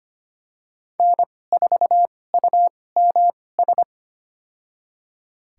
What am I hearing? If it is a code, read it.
Morse code, heard as a keyed tone, about 25 wpm, spaced as for 18 wpm, 700 Hz. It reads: N4UMS
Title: USA Call Signs - 5 Characters - 2x Character Spacing - 25wpm